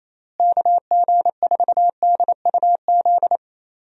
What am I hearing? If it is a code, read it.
Morse code, heard as a keyed tone, about 28 wpm, 700 Hz. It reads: KG4DUZ
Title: USA Call Signs - 6 Characters - Repeated 3x - 28wpm